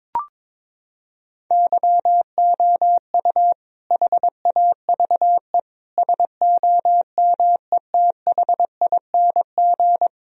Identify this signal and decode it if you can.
Morse code, heard as a keyed tone, about 22 wpm, 700 Hz. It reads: YOU HAVE SOMETHING
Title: Sentences from Top 1000 Words - Review Part 4 - 22wpm